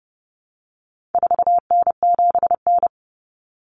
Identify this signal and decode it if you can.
Morse code, heard as a keyed tone, about 30 wpm, 700 Hz. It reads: 4N7N
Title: Common Cut Numbers - 4-Digit 30wpm